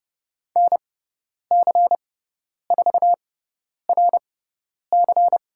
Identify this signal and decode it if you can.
Morse code, heard as a keyed tone, about 30 wpm, spaced as for 10 wpm, 700 Hz. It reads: NC4RC
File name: Lesson 16g - Letter c - Call Signs 30wpm